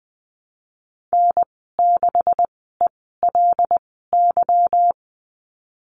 Morse code, heard as a keyed tone, about 20 wpm, spaced as for 14 wpm, 700 Hz. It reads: N6ELY
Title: 161 Lesson 26g - Letter g - Call Signs 20wpm